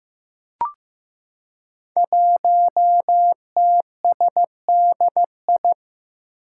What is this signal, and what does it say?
Morse code, heard as a keyed tone, about 15 wpm, 700 Hz. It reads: 1TSDI
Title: Five Challenging Letter-Number Combinations 15wpm